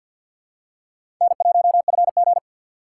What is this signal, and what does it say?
Morse code, heard as a keyed tone, about 50 wpm, 700 Hz. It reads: N1FG
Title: USA Call Signs - 4 Characters - 3x Character Spacing - 50wpm